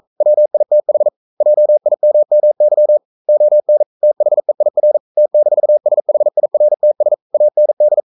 Morse code, heard as a keyed tone, about 42 wpm, 600 Hz. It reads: WITH JIMMY ON THEIR T-SHIRTS AND
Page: Romanen Never af Ken Follet på engelsk.